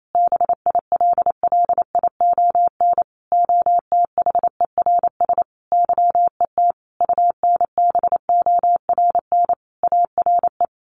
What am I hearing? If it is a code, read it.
Morse code, heard as a keyed tone, about 28 wpm, 700 Hz. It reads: BILLION OTHERS YET UNBORN ARE
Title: A Sound Of Thunder 28wpm